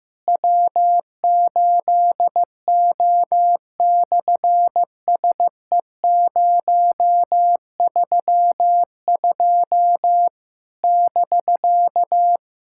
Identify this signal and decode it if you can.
Morse code, heard as a keyed tone, about 15 wpm, 700 Hz. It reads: W8O/SE032 <BK>
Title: SOTA QSO Practice 15wpm